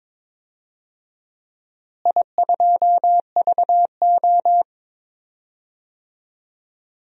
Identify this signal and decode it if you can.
Morse code, heard as a keyed tone, about 22 wpm, 700 Hz. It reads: I2VO
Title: DXCC-100 Call Signs - 4 Characters - Repeated 2x - 22wpm